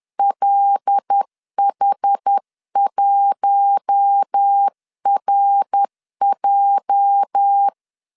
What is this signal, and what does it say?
Morse code, heard as a keyed tone, about 10 wpm, 800 Hz. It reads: LH1RJ
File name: LEZIONI / 9 28 10wpm 2842